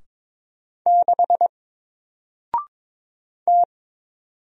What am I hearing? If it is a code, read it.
Morse code, heard as a keyed tone, about 22 wpm, 700 Hz. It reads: 6 T
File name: Single-Letter-Number-22wpm